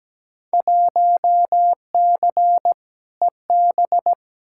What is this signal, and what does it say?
Morse code, heard as a keyed tone, about 17 wpm, 700 Hz. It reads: 1C EB
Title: Field Day Exchanges Encore - 17wpm